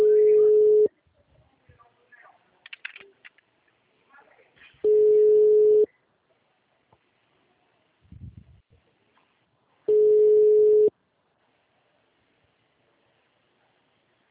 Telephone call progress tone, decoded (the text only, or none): ringback tone